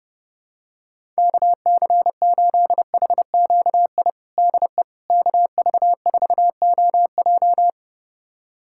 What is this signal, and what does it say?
Morse code, heard as a keyed tone, about 30 wpm, 700 Hz. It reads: KC8HQI DE KV4OJ
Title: USA Callsign DE Callsign 30wpm